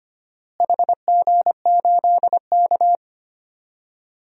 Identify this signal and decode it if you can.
Morse code, heard as a keyed tone, about 25 wpm, 700 Hz. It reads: HG8K
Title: DXCC-100 Call Signs - 4 Characters - Repeated 3x - 25wpm